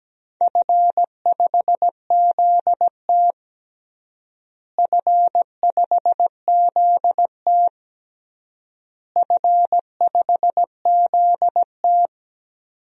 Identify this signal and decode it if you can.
Morse code, heard as a keyed tone, about 17 wpm, 700 Hz. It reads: F5ZT F5ZT F5ZT
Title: DXCC-100 Call Signs - 4 Characters - Repeated 3x - 17wpm